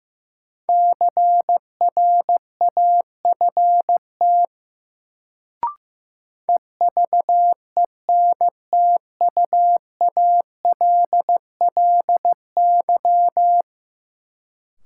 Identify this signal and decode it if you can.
Morse code, heard as a keyed tone, about 15 wpm, 700 Hz. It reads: CRAFT EVENTUALLY
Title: Sentences from Top 1000 Words - Review Part 6 - New Word Drill - 15wpm